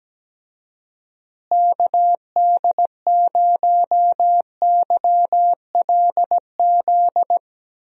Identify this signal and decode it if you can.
Morse code, heard as a keyed tone, about 17 wpm, 700 Hz. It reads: KD0YLZ
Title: USA Call Signs - 6 Characters - Repeated 2x - 17wpm